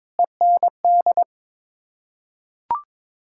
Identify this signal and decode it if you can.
Morse code, heard as a keyed tone, about 21 wpm, 700 Hz. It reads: END
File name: practice / 22wpm / Sets of 8 Words - 22wpm